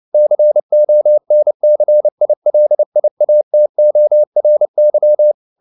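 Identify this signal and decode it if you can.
Morse code, heard as a keyed tone, about 29 wpm, 600 Hz. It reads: CONCILIATORY
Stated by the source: Morse code at 29 wpm, sending Ken Follett's novel Never in English, from CW med Gnister.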